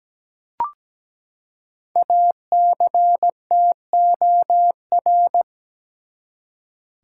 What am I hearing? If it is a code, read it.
Morse code, heard as a keyed tone, about 17 wpm, 700 Hz. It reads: ACTOR
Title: Five Letter Words - 17wpm